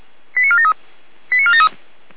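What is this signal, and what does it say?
Five-tone selective call (ZVEI), heard as quick runs of five tones, tones of about 70 ms. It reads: 89473 89473
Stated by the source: ZVEI five-tone selective call received by SDR, from feuerwehr-orlamuende.org (fire brigade Orlamünde).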